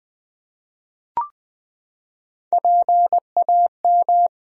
Morse code, heard as a keyed tone, about 20 wpm, 700 Hz. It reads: PAM
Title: USA Common Names - Max 3 Letters - 20wpm